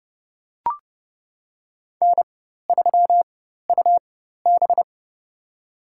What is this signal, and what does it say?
Morse code, heard as a keyed tone, about 30 wpm, spaced as for 14 wpm, 700 Hz. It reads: N3UB
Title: USA Call Signs - 4 Characters - 4x Character Spacing - 30wpm